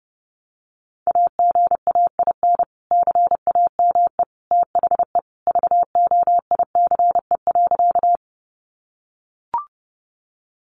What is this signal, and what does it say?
Morse code, heard as a keyed tone, about 30 wpm, 700 Hz. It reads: AGAIN CAME THE VOICE.
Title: Sentences from Top 500 Words 30wpm